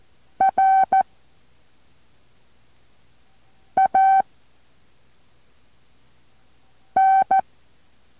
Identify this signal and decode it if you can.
Morse code, heard as a keyed tone, about 14 wpm, 770 Hz. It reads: R A N
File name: intl morse code2